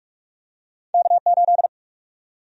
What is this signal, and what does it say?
Morse code, heard as a keyed tone, about 44 wpm, 700 Hz. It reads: K8
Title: USA Call Sign Prefixes - 45wpm